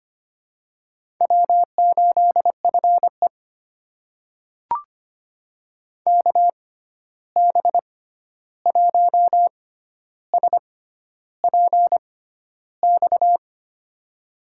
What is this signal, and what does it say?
Morse code, heard as a keyed tone, about 25 wpm, spaced as for 11 wpm, 700 Hz. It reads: W8FE KB1SPX
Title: USA Call Signs - 6x Character Spacing - 25wpm